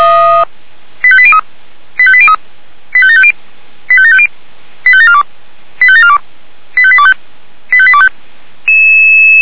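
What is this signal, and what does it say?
Five-tone selective call (ZVEI), heard as quick runs of five tones, tones of about 70 ms. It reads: 85803 85803 85750 85750 85742 85742 85825 85825
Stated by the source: ZVEI five-tone selective call 85827, received by SDR.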